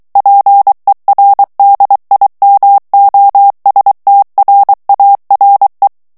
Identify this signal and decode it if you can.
Morse code, heard as a keyed tone, about 23 wpm, 800 Hz. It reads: PER DIMOSTRARE
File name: morse mp3 / ITALIEN texte clair 1 vitesse 22 mots minute